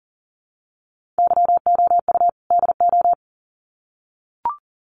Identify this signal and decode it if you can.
Morse code, heard as a keyed tone, about 40 wpm, 700 Hz. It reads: YOU DO
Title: Sets of 2 Words 40wpm